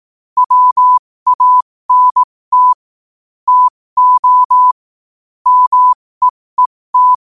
Morse code, heard as a keyed tone, about 18 wpm, spaced as for 15 wpm, 1000 Hz. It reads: WANT TO MEET
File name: qotdfiles / 15wpm